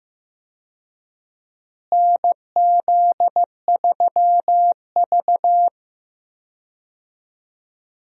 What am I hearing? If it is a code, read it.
Morse code, heard as a keyed tone, about 15 wpm, 700 Hz. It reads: NZ3V
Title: USA Call Signs - 4 Characters - Repeated 3x - 15wpm